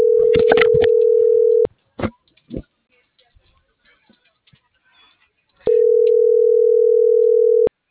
Ringback tone (North America), 440 + 480 Hz.